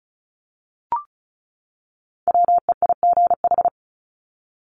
Morse code, heard as a keyed tone, about 34 wpm, 700 Hz. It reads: WEIGH
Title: Top 1751-2000 Words 35wpm